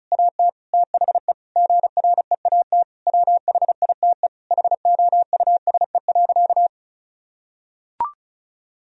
Morse code, heard as a keyed tone, about 36 wpm, 700 Hz. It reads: AT THE GREAT WHITE HOUSE.
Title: Sentences from Top 1000 Words - Review Part 1 - 35wpm